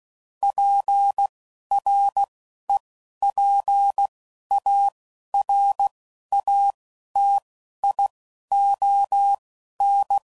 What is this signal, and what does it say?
Morse code, heard as a keyed tone, about 16 wpm, spaced as for 11 wpm, 800 Hz. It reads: PREPARATION